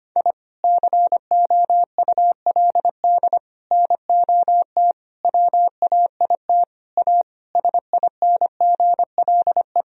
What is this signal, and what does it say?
Morse code, heard as a keyed tone, about 25 wpm, 700 Hz. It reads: I COULD NOT WAIT A SINGLE
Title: Sentences from Top 1000 Words - Review Part 4 - 25wpm